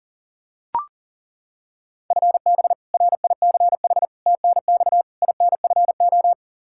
Morse code, heard as a keyed tone, about 40 wpm, 700 Hz. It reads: FB RICH TNX INFO
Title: Practice QSO - Exchange 3 - More Info - 40wpm